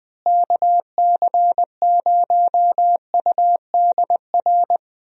Morse code, heard as a keyed tone, about 21 wpm, 700 Hz. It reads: KC0UDR
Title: USA Call Signs - 6 Characters - Repeated 2x - 20wpm